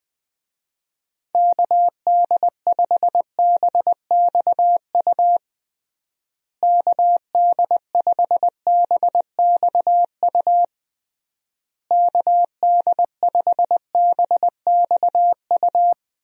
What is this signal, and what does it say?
Morse code, heard as a keyed tone, about 20 wpm, 700 Hz. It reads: KD5BXU KD5BXU KD5BXU
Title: USA Call Signs - 6 Characters - Repeated 3x - 20wpm